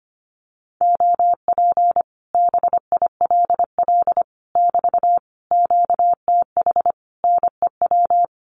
Morse code, heard as a keyed tone, about 25 wpm, 700 Hz. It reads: OP BILL <BT> QTH NEW
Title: Common QSO Phrases Encore 25wpm